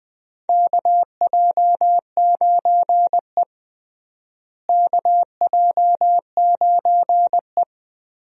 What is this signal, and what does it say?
Morse code, heard as a keyed tone, about 21 wpm, 700 Hz. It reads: KJ9E KJ9E
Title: USA Call Signs - 4 Characters - Repeated 3x - 20wpm